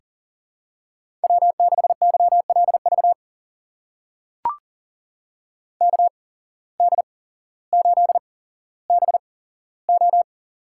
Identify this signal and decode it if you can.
Morse code, heard as a keyed tone, about 40 wpm, spaced as for 17 wpm, 700 Hz. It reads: W6YLV KD8BO
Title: USA Call Signs - 8x Character Spacing - 40wpm